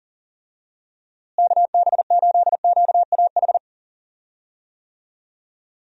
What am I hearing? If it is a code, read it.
Morse code, heard as a keyed tone, about 37 wpm, 700 Hz. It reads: KB8QAH
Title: USA Call Signs - 6 Characters - Repeated 2x - 40wpm